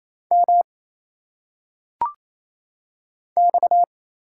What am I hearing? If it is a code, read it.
Morse code, heard as a keyed tone, about 27 wpm, 700 Hz. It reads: M X
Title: Single Letters - Phonetic Pronunciation 28wpm